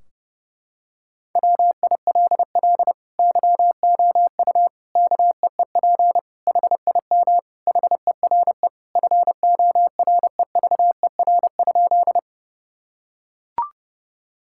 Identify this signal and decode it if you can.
Morse code, heard as a keyed tone, about 31 wpm, 700 Hz. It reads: WILL YOU KEEP HIM HERE FOREVER?
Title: Sentences from Top 1600 Words 30wpm